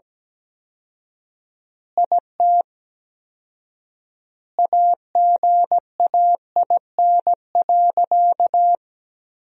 Morse code, heard as a keyed tone, about 17 wpm, 700 Hz. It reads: IT AGAIN.
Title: 4x Word-Spacing - Sentences from Top 300 Words - 17wpm